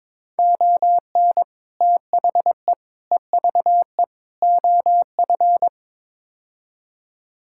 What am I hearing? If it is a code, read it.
Morse code, heard as a keyed tone, about 22 wpm, 700 Hz. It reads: ON THE EVE OF
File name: practice / 22wpm / Build A Phrase - 5-Word Phrases - 22wpm